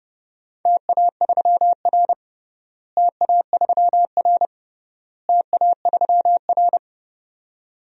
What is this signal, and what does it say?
Morse code, heard as a keyed tone, about 30 wpm, 700 Hz. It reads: TA3R TA3R TA3R
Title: DXCC-100 Call Signs - 4 Characters - Repeated 3x - 30wpm